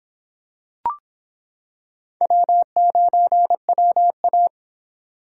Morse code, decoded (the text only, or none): W9WA